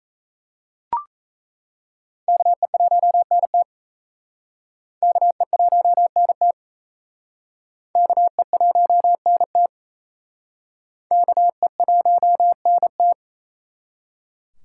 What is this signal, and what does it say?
Morse code, heard as a keyed tone, about 37 wpm, 700 Hz. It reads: KE1NT KE1NT KE1NT KE1NT